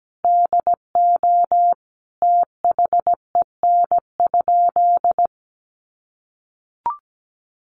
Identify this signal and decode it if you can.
Morse code, heard as a keyed tone, about 17 wpm, 700 Hz. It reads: DO THEN?
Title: Sentences from Top 100 Words 17wpm